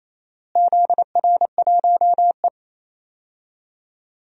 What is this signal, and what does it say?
Morse code, heard as a keyed tone, about 28 wpm, 700 Hz. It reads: ZR1E
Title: DXCC-100 Call Signs - 4 Characters - Repeated 2x - 28wpm